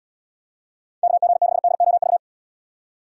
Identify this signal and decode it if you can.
Morse code, heard as a keyed tone, about 78 wpm, 700 Hz. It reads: KG6DOV